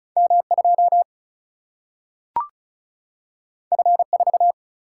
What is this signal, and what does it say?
Morse code, heard as a keyed tone, about 36 wpm, 700 Hz. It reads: M 2 F 4
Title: Single-Letter Single-Number 35wpm